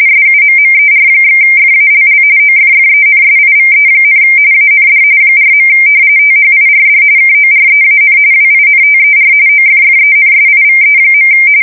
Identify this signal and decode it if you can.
RTTY, about 45.45 baud. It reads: SUMMER/FALL OFFENSIVE, AND 4. THIRD KOREAN WINTER OFFENSIVE.
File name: fsk170 / battleshipnj